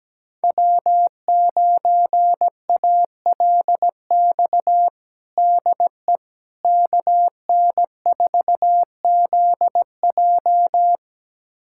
Morse code, heard as a keyed tone, about 17 wpm, 700 Hz. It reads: W9ALX DE KN4ZJ
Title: USA Callsign DE Callsign - 5 Characters - 17wpm